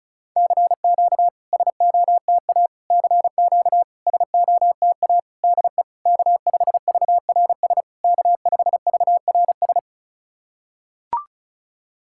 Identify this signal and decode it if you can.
Morse code, heard as a keyed tone, about 36 wpm, 700 Hz. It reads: CQ SOTA CQ SOTA DE K5VRS K5VRS
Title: SOTA QSO Practice 35wpm